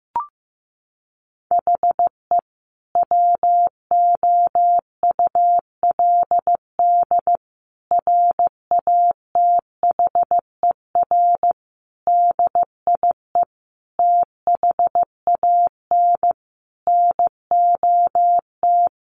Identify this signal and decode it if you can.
Morse code, heard as a keyed tone, about 14 wpm, 700 Hz. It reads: HE WOULD RATHER DIE THAN NOT
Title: Sentences from Top 1500 Words 15wpm